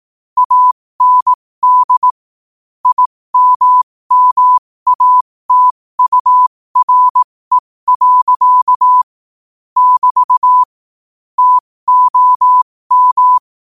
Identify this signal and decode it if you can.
Morse code, heard as a keyed tone, about 18 wpm, spaced as for 15 wpm, 1000 Hz. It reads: AND IMMATURE. <BT> TOM
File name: qotdfiles / 15wpm